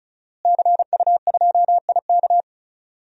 Morse code, heard as a keyed tone, about 34 wpm, 700 Hz. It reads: CU2IK
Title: DXCC-100 Call Signs - 5 Characters - Repeated 2x - 35wpm